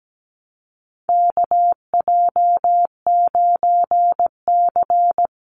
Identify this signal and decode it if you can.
Morse code, heard as a keyed tone, about 17 wpm, 700 Hz. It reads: KJ9C
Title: Call Signs Repeated 3x - 17wpm